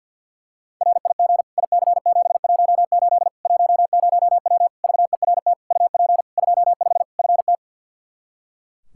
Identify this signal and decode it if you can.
Morse code, heard as a keyed tone, about 46 wpm, 700 Hz. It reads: RIG IC718 10W VERT UP 25 FT
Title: Practice QSO - Exchange 2 - Info - 50wpm